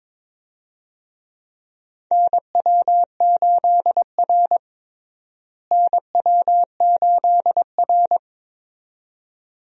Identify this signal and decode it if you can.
Morse code, heard as a keyed tone, about 22 wpm, 700 Hz. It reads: NW8R NW8R
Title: USA Call Signs - 4 Characters - Repeated 2x - 22wpm